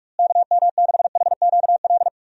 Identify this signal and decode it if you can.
Morse code, heard as a keyed tone, about 41 wpm, 700 Hz. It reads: KM6HQL
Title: USA Call Signs - 6 Characters - Repeated 3x - 45wpm